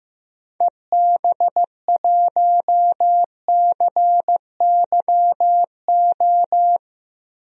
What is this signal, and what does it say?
Morse code, heard as a keyed tone, about 15 wpm, 700 Hz. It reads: EB1CYO
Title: DXCC-100 Call Signs - 6 Characters - 15wpm